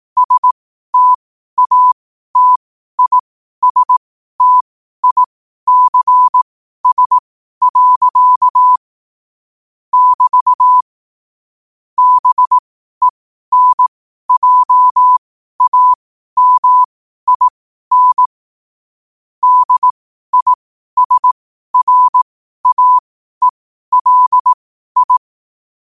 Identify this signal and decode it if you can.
Morse code, heard as a keyed tone, about 18 wpm, spaced as for 12 wpm, 1000 Hz. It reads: STATISTICS. <BT> BENJAMIN DISRAELI
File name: qotdfiles / 12wpm